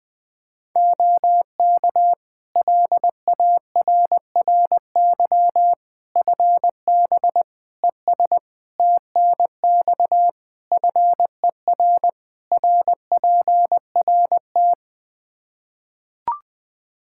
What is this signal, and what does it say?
Morse code, heard as a keyed tone, about 20 wpm, 700 Hz. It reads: OK LARRY FB ES TNX FER RPRT
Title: Practice QSO - Exchange 2 - Info - 20wpm